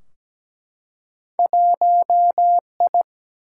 Morse code, heard as a keyed tone, about 17 wpm, 700 Hz. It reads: 1I
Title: Single-Letter Single-Number 17wpm